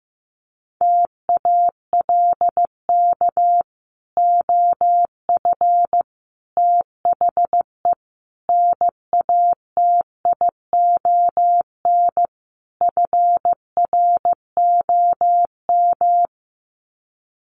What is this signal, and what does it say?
Morse code, heard as a keyed tone, about 15 wpm, 700 Hz. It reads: TALK OF THE NATION FROM
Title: Sets of 5 Words 15wpm